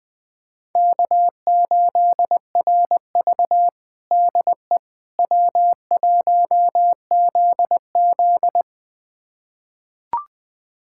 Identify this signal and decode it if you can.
Morse code, heard as a keyed tone, about 20 wpm, 700 Hz. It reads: K8RV DE W1ZZ
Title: USA Callsign DE Callsign - 4 Characters - 20wpm